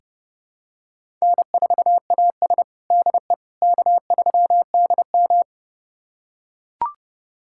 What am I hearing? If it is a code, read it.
Morse code, heard as a keyed tone, about 30 wpm, 700 Hz. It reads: N4AS DE K3DM
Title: USA Callsign DE Callsign - 4 Characters - 30wpm